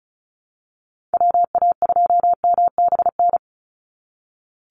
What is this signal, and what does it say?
Morse code, heard as a keyed tone, about 34 wpm, 700 Hz. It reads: WA2MBN
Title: USA Call Signs 35wpm